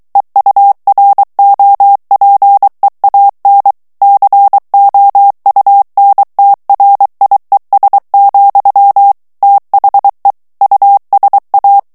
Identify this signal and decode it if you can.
Morse code, heard as a keyed tone, about 23 wpm, 800 Hz. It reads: EUROPEAN COUNTRIES, THE USA
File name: morse mp3 / ANGLAIS texte clair 2 vitesse 22 mots minute